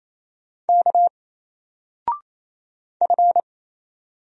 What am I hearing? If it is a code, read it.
Morse code, heard as a keyed tone, about 28 wpm, 700 Hz. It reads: K F
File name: Single-Letter-Number-28wpm